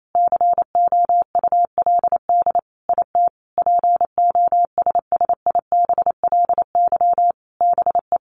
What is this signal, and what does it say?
Morse code, heard as a keyed tone, about 28 wpm, 700 Hz. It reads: COULD IT POSSIBLY BE
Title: Sentences from Top 1300 Words 28wpm